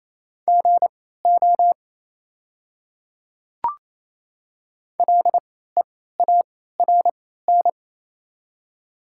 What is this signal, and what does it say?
Morse code, heard as a keyed tone, about 28 wpm, spaced as for 16 wpm, 700 Hz. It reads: GO LEARN
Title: 3x Character Spacing - Top 300 Words - 28wpm